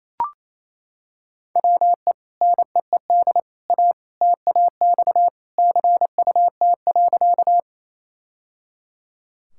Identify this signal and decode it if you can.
Morse code, heard as a keyed tone, about 27 wpm, 700 Hz. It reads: WE NEED A TAX CUT.
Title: Sentences from Top 1000 Words - Review Part 5 - 28wpm